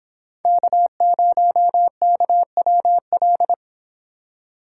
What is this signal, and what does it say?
Morse code, heard as a keyed tone, about 29 wpm, 700 Hz. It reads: K0KWL